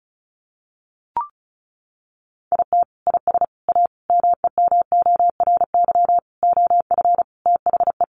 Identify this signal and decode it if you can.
Morse code, heard as a keyed tone, about 34 wpm, 700 Hz. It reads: IT IS A MEMORY OF THE
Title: Sentences from Top 1300 Words 35wpm